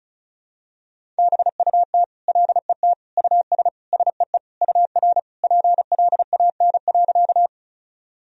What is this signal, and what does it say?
Morse code, heard as a keyed tone, about 34 wpm, 700 Hz. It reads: BUT LET US SEE UR PLAN.
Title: Sentences from Top 500 Words and Abbreviations 35wpm